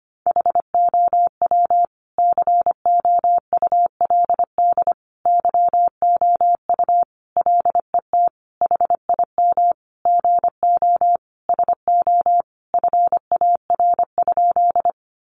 Morse code, heard as a keyed tone, about 25 wpm, 700 Hz. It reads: HOW COULD YOU LET HIM GO SO FAR?
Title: Sentences from Top 300 Words 25wpm